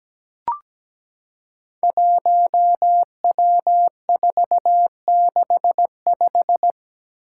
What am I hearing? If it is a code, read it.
Morse code, heard as a keyed tone, about 17 wpm, 700 Hz. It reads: 1W465
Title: Five Challenging Letter-Number Combinations 17wpm